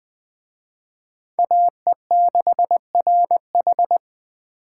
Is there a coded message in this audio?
Yes: Morse code from 1.4 to 4.0 s, heard as a keyed tone, about 21 wpm, 700 Hz. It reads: AE6RH